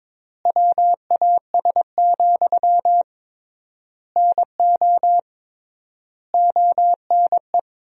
Morse code, heard as a keyed tone, about 22 wpm, 700 Hz. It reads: WAS, NO ONE
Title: 3x Word-Spacing - Sentences from Top 400 Words - 22wpm